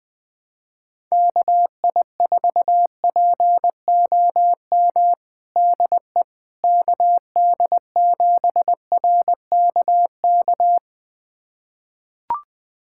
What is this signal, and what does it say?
Morse code, heard as a keyed tone, about 20 wpm, 700 Hz. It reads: KI4POM DE KD7RKK
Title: USA Callsign DE Callsign - 6 Characters - 20wpm